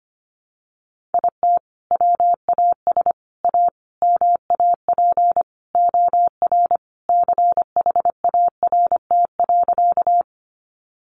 Morse code, heard as a keyed tone, about 25 wpm, 700 Hz. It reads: IT WAS A MAP OR CHART.
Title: Sentences from Top 1300 Words 25wpm